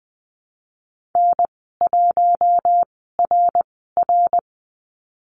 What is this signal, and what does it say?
Morse code, heard as a keyed tone, about 20 wpm, spaced as for 15 wpm, 700 Hz. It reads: N1RR
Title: 069 Lesson 12g - Letter d - Call Signs 20wpm